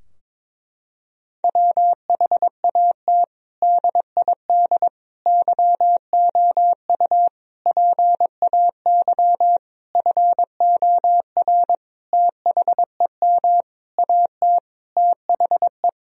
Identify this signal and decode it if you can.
Morse code, heard as a keyed tone, about 22 wpm, 700 Hz. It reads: WHAT DID YOU PAY FOR THEM AT THE
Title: Sentences from Top 1000 Words - Review Part 3 - 22wpm